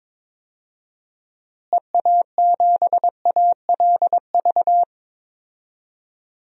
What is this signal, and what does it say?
Morse code, heard as a keyed tone, about 22 wpm, 700 Hz. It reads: EA7ALV
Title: DXCC-100 Call Signs - 6 Characters - Repeated 3x - 22wpm